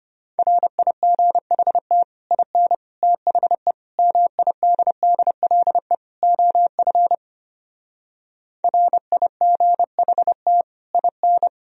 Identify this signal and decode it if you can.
Morse code, heard as a keyed tone, about 33 wpm, 700 Hz. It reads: RIGHT IN THE MIDDLE OF RIGHT IN